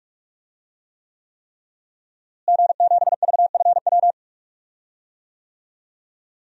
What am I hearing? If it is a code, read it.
Morse code, heard as a keyed tone, about 46 wpm, 700 Hz. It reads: G7VFW
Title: DXCC-100 Call Signs - 5 Characters - Repeated 3x - 45wpm